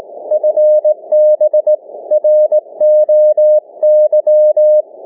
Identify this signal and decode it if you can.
Morse code, heard as a keyed tone, about 20 wpm, 610 Hz. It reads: FB ROY
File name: cw / ssn 30M